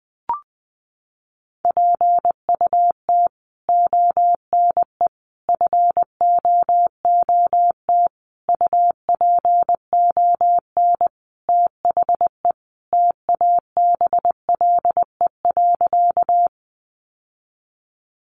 Morse code, heard as a keyed tone, about 20 wpm, 700 Hz. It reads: PUT ONE FOOT UPON THE TABLE.
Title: Sentences from Top 1100 Words 20wpm